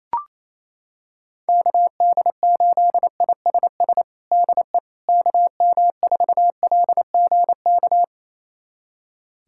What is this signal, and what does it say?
Morse code, heard as a keyed tone, about 28 wpm, 700 Hz. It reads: KD8ISS DE KM4LGK
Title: USA Callsign DE Callsign - 6 Characters - 28wpm